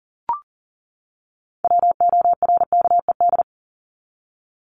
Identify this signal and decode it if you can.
Morse code, heard as a keyed tone, about 40 wpm, 700 Hz. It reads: WORKED
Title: ED Suffix - Top 100 Words with Base Words 40wpm